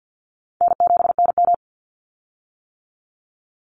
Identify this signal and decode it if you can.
Morse code, heard as a keyed tone, about 50 wpm, 700 Hz. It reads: N7NM
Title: Call Signs - 50wpm